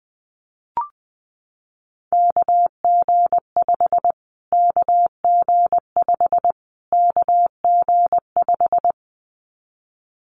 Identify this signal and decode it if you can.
Morse code, heard as a keyed tone, about 20 wpm, 700 Hz. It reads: KG5 KG5 KG5
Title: Callsign Prefixes for Top 25 DXCC - Repeated 3x - 20wpm